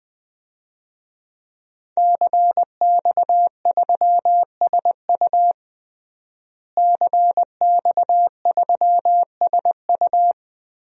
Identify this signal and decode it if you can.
Morse code, heard as a keyed tone, about 21 wpm, 700 Hz. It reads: CX3SU CX3SU
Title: DXCC-100 Call Signs - 5 Characters - Repeated 3x - 20wpm